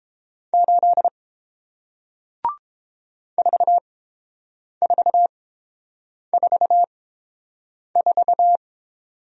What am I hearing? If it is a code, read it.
Morse code, heard as a keyed tone, about 30 wpm, spaced as for 7 wpm, 700 Hz. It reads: 8 4444